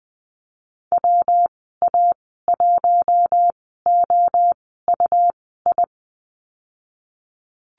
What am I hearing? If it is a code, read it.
Morse code, heard as a keyed tone, about 20 wpm, spaced as for 15 wpm, 700 Hz. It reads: WA1OUI
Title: 167 Lesson 27g - Number 7 - Call Signs 20wpm